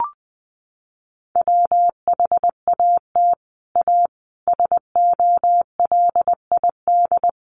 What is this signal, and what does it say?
Morse code, heard as a keyed tone, about 20 wpm, 700 Hz. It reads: WHAT A SOLID
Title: Sentences from Top 1500 Words 20wpm